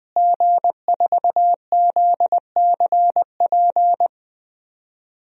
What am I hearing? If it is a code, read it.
Morse code, heard as a keyed tone, about 20 wpm, 700 Hz. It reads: G4ZCP
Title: DXCC-100 Call Signs - 5 Characters - Repeated 2x - 20wpm